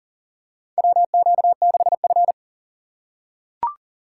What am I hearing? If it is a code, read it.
Morse code, heard as a keyed tone, about 40 wpm, 700 Hz. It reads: WQ6F
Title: USA Call Signs - 4 Characters - 6x Character Spacing - 40wpm